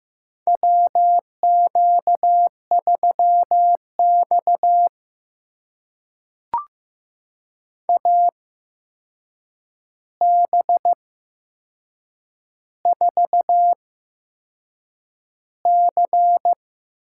Morse code, heard as a keyed tone, about 15 wpm, spaced as for 6 wpm, 700 Hz. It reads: WQ3X AB4C
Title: USA Call Signs - 8x Character Spacing - 15wpm